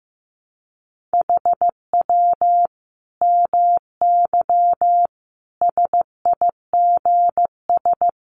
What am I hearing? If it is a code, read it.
Morse code, heard as a keyed tone, about 15 wpm, 700 Hz. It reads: HW MY SIGS
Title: Common QSO Phrases 15wpm